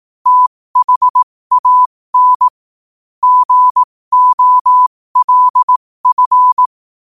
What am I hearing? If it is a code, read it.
Morse code, heard as a keyed tone, about 18 wpm, spaced as for 15 wpm, 1000 Hz. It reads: THAN GOLF
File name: qotdfiles / 15wpm